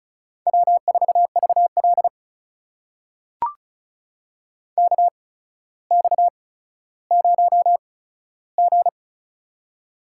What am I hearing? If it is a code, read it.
Morse code, heard as a keyed tone, about 36 wpm, spaced as for 15 wpm, 700 Hz. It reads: W4VL KX0G
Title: USA Call Signs - 4 Characters - 8x Character Spacing - 35wpm